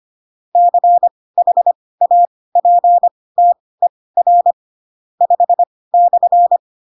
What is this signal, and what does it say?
Morse code, heard as a keyed tone, about 25 wpm, spaced as for 18 wpm, 700 Hz. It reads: CHAPTER 5/
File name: Pinocchio / Pinocchio ch5 12 25 ik1hge